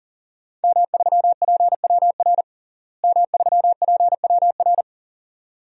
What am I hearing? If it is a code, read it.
Morse code, heard as a keyed tone, about 40 wpm, 700 Hz. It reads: M3PWR M3PWR
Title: DXCC-100 Call Signs - 5 Characters - Repeated 2x - 40wpm